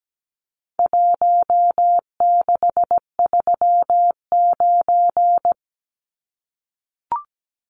Morse code, heard as a keyed tone, about 17 wpm, 700 Hz. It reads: 1639
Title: Numbers - Contest Style - 4-Digit 17wpm